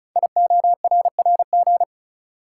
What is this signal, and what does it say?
Morse code, heard as a keyed tone, about 36 wpm, 690 Hz. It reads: IORRG
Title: Code Groups - 5 Letters - 35wpm